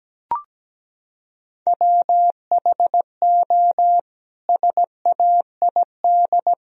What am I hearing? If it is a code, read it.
Morse code, heard as a keyed tone, about 17 wpm, 700 Hz. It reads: WHO SAID
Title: Sentences from Top 1000 Words - Review Part 4 - 17wpm